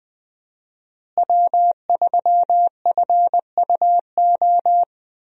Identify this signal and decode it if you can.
Morse code, heard as a keyed tone, about 20 wpm, 700 Hz. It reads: W3FUO
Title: USA Call Signs - 5 Characters - Repeated 3x - 20wpm